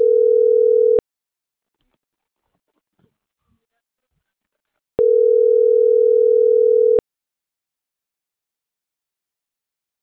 Ringback tone (North America), 440 + 480 Hz.